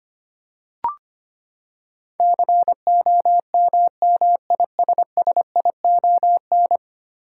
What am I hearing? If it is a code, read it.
Morse code, heard as a keyed tone, about 25 wpm, 700 Hz. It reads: COMMISSION
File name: Ten Letter Words - 25wpm